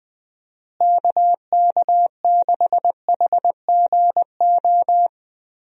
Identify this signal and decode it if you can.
Morse code, heard as a keyed tone, about 21 wpm, 700 Hz. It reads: KK6HGO
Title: USA Call Signs - 6 Characters - Repeated 2x - 20wpm